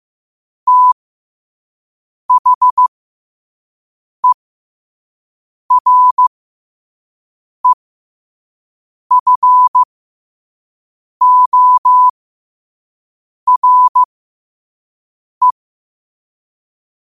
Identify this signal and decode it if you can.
Morse code, heard as a keyed tone, about 15 wpm, spaced as for 5 wpm, 1000 Hz. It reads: THEREFORE
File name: qotdfiles / 5wpm